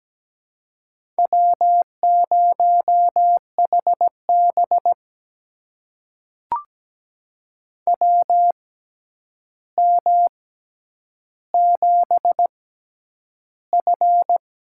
Morse code, heard as a keyed tone, about 17 wpm, spaced as for 9 wpm, 700 Hz. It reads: W0HB WM7F
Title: USA Call Signs - 4 Characters - 6x Character Spacing - 17wpm